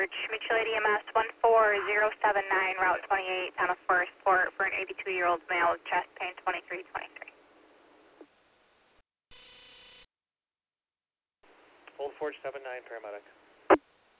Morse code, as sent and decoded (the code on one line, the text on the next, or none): none